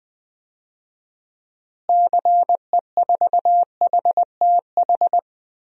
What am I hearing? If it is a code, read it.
Morse code, heard as a keyed tone, about 20 wpm, 690 Hz. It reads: CE4HTH
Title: DXCC-100 Call Signs - 6 Characters - Repeated 2x - 20wpm